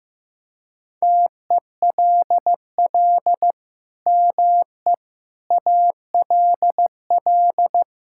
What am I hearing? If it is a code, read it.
Morse code, heard as a keyed tone, about 15 wpm, 700 Hz. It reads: TELL ME ALL
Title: Sentences from Top 1000 Words - Review Part 6 - 15wpm